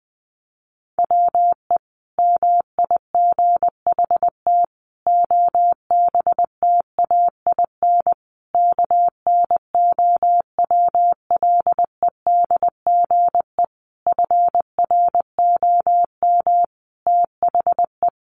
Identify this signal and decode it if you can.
Morse code, heard as a keyed tone, about 20 wpm, 700 Hz. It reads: WE MIGHT OBTAIN KNOWLEDGE FROM THE
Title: Sentences from Top 1300 Words 20wpm